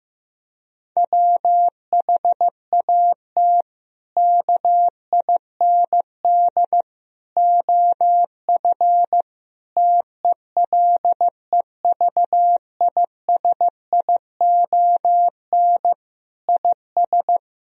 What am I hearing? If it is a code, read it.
Morse code, heard as a keyed tone, about 15 wpm, 700 Hz. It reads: WHAT KIND OF TELEVISION IS
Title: Sentences from Top 1000 Words - Review Part 4 - 15wpm